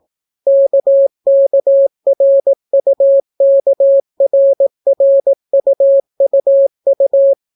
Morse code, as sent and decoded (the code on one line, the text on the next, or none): -.- -.- .-. ..- -.- .-. .-. ..- ..- ..-
KKRUKRRUUU